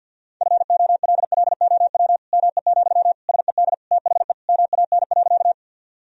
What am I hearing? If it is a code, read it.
Morse code, heard as a keyed tone, about 52 wpm, 700 Hz. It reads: FOLLOW ME, SED THE MAN.
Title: Sentences from Top 200 Words and Abbreviations 50wpm